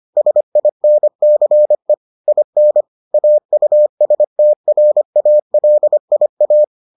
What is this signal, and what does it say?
Morse code, heard as a keyed tone, about 25 wpm, 600 Hz. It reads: SINCE IN AUSTRALIA